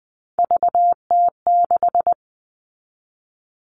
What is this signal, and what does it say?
Morse code, heard as a keyed tone, about 20 wpm, 700 Hz. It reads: VT6
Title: All Cut Numbers - 3-Digit 20wpm